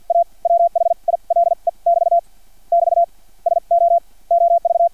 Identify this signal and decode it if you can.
Morse code, heard as a keyed tone, about 50 wpm, spaced as for 40 wpm, 680 Hz. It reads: A WHILE <BT> <BT> SO OV
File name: archived / CRSnet